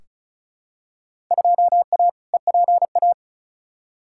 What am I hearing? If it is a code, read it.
Morse code, heard as a keyed tone, about 34 wpm, 700 Hz. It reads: 2A EPA
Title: Field Day Exchanges - 35wpm